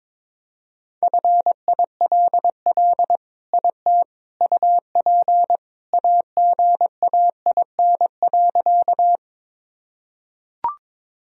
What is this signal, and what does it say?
Morse code, heard as a keyed tone, about 22 wpm, 700 Hz. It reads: FILL IT UP AGAIN.
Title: Sentences from Top 1000 Words - Review Part 3 - 22wpm